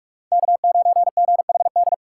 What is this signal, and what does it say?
Morse code, heard as a keyed tone, about 46 wpm, 700 Hz. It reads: K9GHD
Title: USA Call Signs - 6x Character Spacing - 45wpm